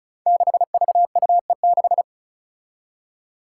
Morse code, heard as a keyed tone, about 36 wpm, 700 Hz. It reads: 6VUE6
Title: Five Challenging Letter-Number Combinations 35wpm